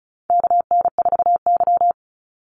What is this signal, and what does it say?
Morse code, heard as a keyed tone, about 34 wpm, 700 Hz. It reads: KN4Y
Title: Call Signs Repeated 3x - 35wpm